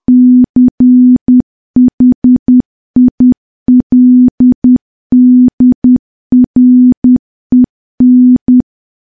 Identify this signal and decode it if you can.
Morse code, heard as a keyed tone, about 10 wpm, 250 Hz. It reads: CHILDREN
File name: morse / morse-training-10WPM